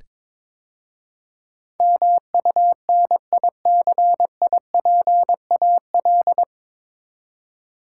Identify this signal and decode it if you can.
Morse code, heard as a keyed tone, about 21 wpm, 700 Hz. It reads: MUNICIPAL
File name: 9 Letter Words - First 3 Letters Given - 22wpm